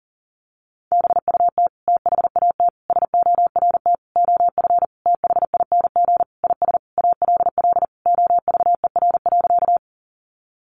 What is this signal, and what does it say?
Morse code, heard as a keyed tone, about 40 wpm, 700 Hz. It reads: BUT THAT SORT OF THING IS ALL OVER.
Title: Sentences from Top 1300 Words 40wpm